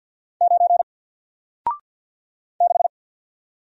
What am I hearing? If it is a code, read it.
Morse code, heard as a keyed tone, about 54 wpm, 700 Hz. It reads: 9 6
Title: Single Letter-Number Alt 50wpm